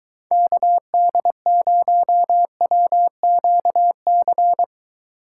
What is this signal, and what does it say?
Morse code, heard as a keyed tone, about 24 wpm, 700 Hz. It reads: KD0WQC